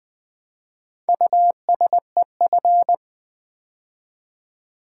Morse code, heard as a keyed tone, about 20 wpm, 700 Hz. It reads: USEF
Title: Build A Word - 2 Letters At A Time - 6-Letter Words - 20wpm